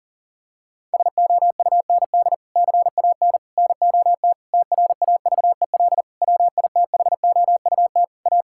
Morse code, heard as a keyed tone, about 40 wpm, 700 Hz. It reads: SOUND CAN NOT TRAVEL WITHOUT A
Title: Sentences from Top 1000 Words - Review Part 6 - 40wpm